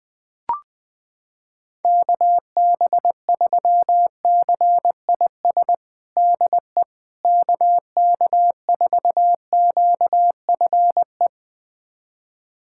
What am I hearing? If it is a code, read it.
Morse code, heard as a keyed tone, about 20 wpm, 700 Hz. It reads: KB3CIS DE KK4QFE
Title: USA Callsign DE Callsign - 6 Characters - 20wpm